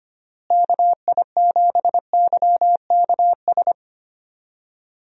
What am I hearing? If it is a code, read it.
Morse code, heard as a keyed tone, about 25 wpm, 700 Hz. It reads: KI7YKS
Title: USA Call Signs - 6 Characters - Repeated 3x - 25wpm